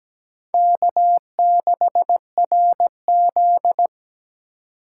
Morse code, heard as a keyed tone, about 20 wpm, 700 Hz. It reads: K6RZ
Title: Speed Racing - USA 1x2 Call Signs 17wpm